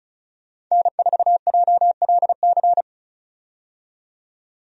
Morse code, heard as a keyed tone, about 36 wpm, 700 Hz. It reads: N4JLC
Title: USA Call Signs - 5 Characters - 35wpm